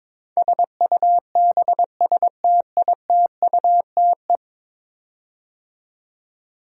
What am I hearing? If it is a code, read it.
Morse code, heard as a keyed tone, about 22 wpm, 700 Hz. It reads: SUBSTITUTE
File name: practice / 22wpm / 10 Letter Words - First 3 Letters Given - 22wpm